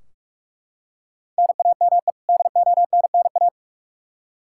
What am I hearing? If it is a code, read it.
Morse code, heard as a keyed tone, about 44 wpm, 700 Hz. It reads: NAME DONNA
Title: Practice QSO - Exchange 1 - Beginning - 45wpm